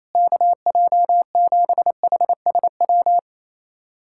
Morse code, heard as a keyed tone, about 27 wpm, 700 Hz. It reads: KJ7HSW
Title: USA Callsign DE Callsign - 6 Characters - 28wpm